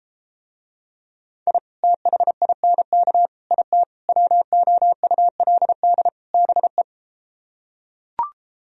Morse code, heard as a keyed tone, about 33 wpm, 700 Hz. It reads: I THINK IT WOULD BE